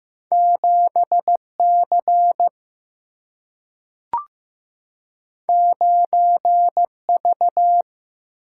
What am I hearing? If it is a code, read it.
Morse code, heard as a keyed tone, about 15 wpm, 700 Hz. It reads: 7C 9V
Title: Single-Letter Single-Number 15wpm